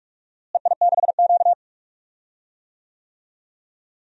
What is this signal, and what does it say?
Morse code, heard as a keyed tone, about 46 wpm, 700 Hz. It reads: EI6Q
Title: DXCC-100 Call Signs - 4 Characters - Repeated 3x - 45wpm